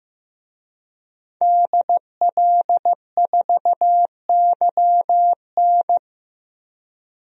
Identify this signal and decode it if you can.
Morse code, heard as a keyed tone, about 15 wpm, 700 Hz. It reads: DL4YN
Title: DXCC-100 Call Signs - 5 Characters - Repeated 2x - 15wpm